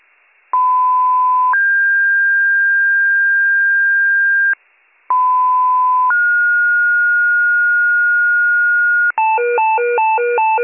Two-tone page, heard as a one-second tone, then a three-second tone, about 1010 then 1600 Hz, about 1010 then 1430 Hz, with noise. It is heard from 0.5 to 9.1 s.